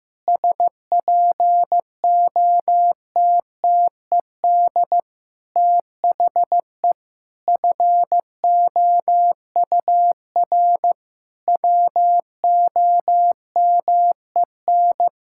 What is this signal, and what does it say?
Morse code, heard as a keyed tone, about 15 wpm, 700 Hz. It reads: SPOTTED THE FOUR WOMEN
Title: Pangram Challenge v2 - 15wpm